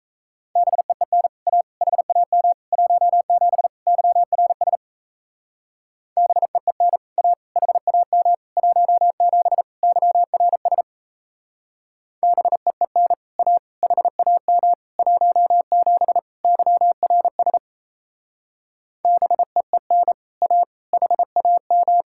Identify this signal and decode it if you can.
Morse code, heard as a keyed tone, about 34 wpm, 700 Hz. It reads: BEEN A HAM 17 YRS BEEN A HAM 17 YRS BEEN A HAM 17 YRS BEEN A HAM
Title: Speed-Racing Common QSO Phrases Encore - 28wpm